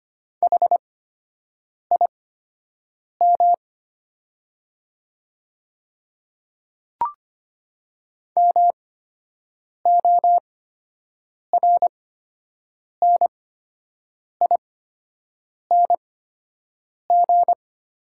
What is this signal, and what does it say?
Morse code, heard as a keyed tone, about 25 wpm, spaced as for 7 wpm, 700 Hz. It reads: HIM MORNING
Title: 8x Character Spacing - Top 500 Words - 25wpm